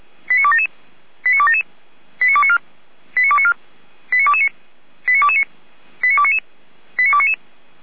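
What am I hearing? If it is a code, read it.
Five-tone selective call (ZVEI), heard as quick runs of five tones, tones of about 70 ms. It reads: 89277 89277 89294 89294 89229 89229 89299 89299
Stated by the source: ZVEI five-tone call received by SDR.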